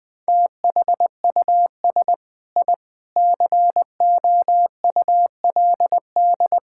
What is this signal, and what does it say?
Morse code, heard as a keyed tone, about 20 wpm, 700 Hz. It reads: THUS I COULD
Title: Sentences from Top 1000 Words - Review Part 5 - 20wpm